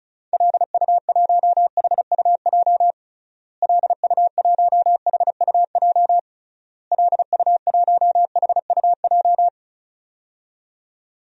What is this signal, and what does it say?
Morse code, heard as a keyed tone, about 36 wpm, 700 Hz. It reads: LU1HUJ LU1HUJ LU1HUJ
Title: DXCC-100 Call Signs - 6 Characters - Repeated 3x - 35wpm